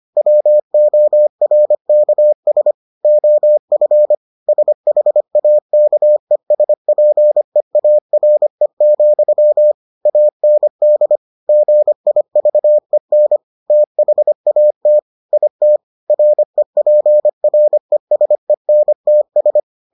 Morse code, heard as a keyed tone, about 25 wpm, 600 Hz. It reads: WORKS OF SHAKESPEARE, AND GIVEN THAT IT REPRESENTS